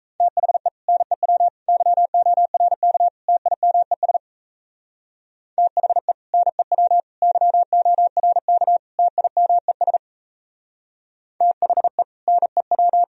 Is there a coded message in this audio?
Yes: Morse code from 0.2 to 13.0 s, heard as a keyed tone, about 37 wpm, 700 Hz. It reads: THE NEW YORK TIMES THE NEW YORK TIMES THE NEW